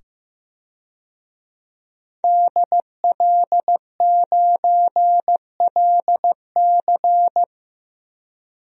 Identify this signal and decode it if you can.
Morse code, heard as a keyed tone, about 15 wpm, 700 Hz. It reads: DL9LC
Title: DXCC-100 Call Signs - 5 Characters - 15wpm